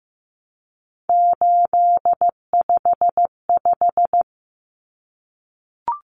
Morse code, heard as a keyed tone, about 15 wpm, 700 Hz. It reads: 855
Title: Numbers - 3-Digit 15wpm